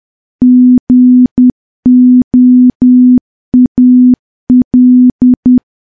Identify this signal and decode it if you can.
Morse code, heard as a keyed tone, about 10 wpm, 250 Hz. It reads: GOAL
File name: morse / morse-training-10WPM